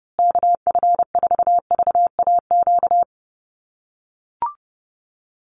Morse code, heard as a keyed tone, about 30 wpm, 700 Hz. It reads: KF4VAQ
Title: USA Call Signs Repeated 3x 30wpm